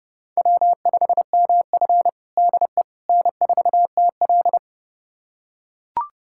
Morse code, heard as a keyed tone, about 30 wpm, 700 Hz. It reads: W5MF DE N4TL
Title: USA Callsign DE Callsign - 4 Characters - 30wpm